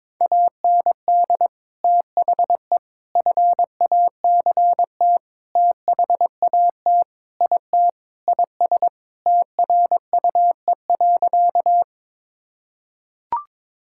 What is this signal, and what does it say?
Morse code, heard as a keyed tone, about 22 wpm, 700 Hz. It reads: AND THE FACT THAT IT IS TRUE.
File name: practice / 22wpm / Sentences with 4-Word Phrases - 22wpm